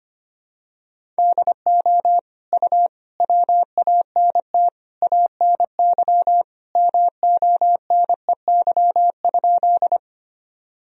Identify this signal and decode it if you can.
Morse code, heard as a keyed tone, about 25 wpm, 700 Hz. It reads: DO U WANT ANY MONEY?
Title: Sentences from Top 400 Words and Abbreviations 25wpm